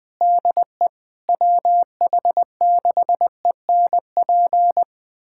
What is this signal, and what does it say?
Morse code, heard as a keyed tone, about 20 wpm, 700 Hz. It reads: DE WH6ENP
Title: USA Callsign DE Callsign - 6 Characters - 20wpm